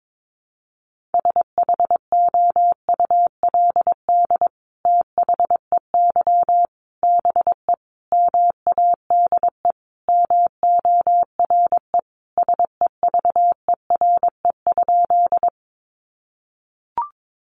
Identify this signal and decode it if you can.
Morse code, heard as a keyed tone, about 22 wpm, 700 Hz. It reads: SHOULD THEY BE MADE MORE SEVERE?
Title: Sentences from Top 1400 Words 22wpm